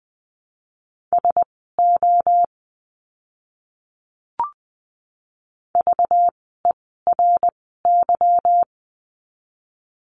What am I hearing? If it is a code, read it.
Morse code, heard as a keyed tone, about 20 wpm, spaced as for 15 wpm, 700 Hz. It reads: SO VERY
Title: 230 Lesson 37f - Letter x - Words 20wpm